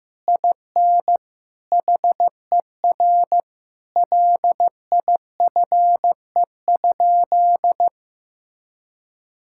Morse code, heard as a keyed tone, about 15 wpm, 700 Hz. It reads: IN HER LIFE?
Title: Sentences from Top 1000 Words - Review Part 6 - 15wpm